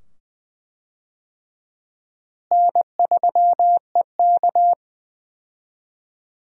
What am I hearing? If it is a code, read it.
Morse code, heard as a keyed tone, about 20 wpm, 700 Hz. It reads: N3EK
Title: USA Call Signs - 4 Characters - Repeated 2x - 20wpm